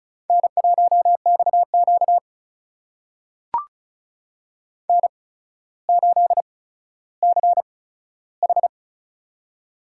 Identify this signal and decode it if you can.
Morse code, heard as a keyed tone, about 36 wpm, spaced as for 15 wpm, 700 Hz. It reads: N1XQ N8CH
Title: USA Call Signs - 4 Characters - 8x Character Spacing - 35wpm